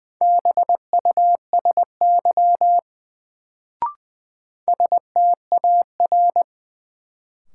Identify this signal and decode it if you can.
Morse code, heard as a keyed tone, about 20 wpm, 700 Hz. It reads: BUSY STAR
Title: Sentences from Top 1000 Words - Review Part 2 - New Word Drill - 20wpm